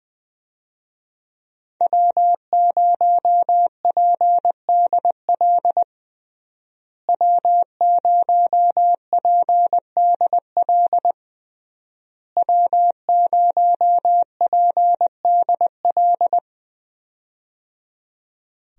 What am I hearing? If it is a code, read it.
Morse code, heard as a keyed tone, about 20 wpm, 700 Hz. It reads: W0PDL W0PDL W0PDL
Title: USA Call Signs - 5 Characters - Repeated 3x - 20wpm